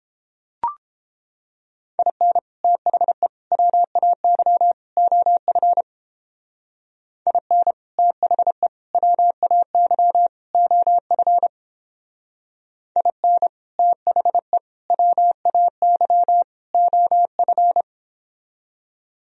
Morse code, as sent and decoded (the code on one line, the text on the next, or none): .. -. / - .... / .-- .- -.-- / --- ..-. / .. -. / - .... / .-- .- -.-- / --- ..-. / .. -. / - .... / .-- .- -.-- / --- ..-.
IN THE WAY OF IN THE WAY OF IN THE WAY OF